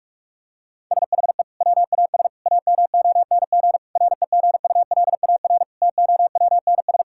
Morse code, heard as a keyed tone, about 41 wpm, 700 Hz. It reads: SHE WAS AMONG REGULAR TOWNS
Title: Sentences from Top 1000 Words - Review Part 6 - 45wpm